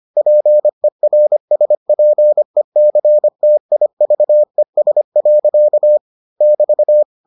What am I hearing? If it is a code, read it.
Morse code, heard as a keyed tone, about 25 wpm, 600 Hz. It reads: PERSPECTIVES. <BT>